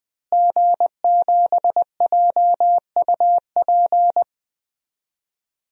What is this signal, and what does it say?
Morse code, heard as a keyed tone, about 21 wpm, 700 Hz. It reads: G7JUP
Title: DXCC-100 Call Signs - 5 Characters - Repeated 2x - 20wpm